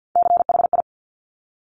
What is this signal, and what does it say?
Morse code, heard as a keyed tone, about 50 wpm, 700 Hz. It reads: CHI
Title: Top Three Letter Combinations in Words 50wpm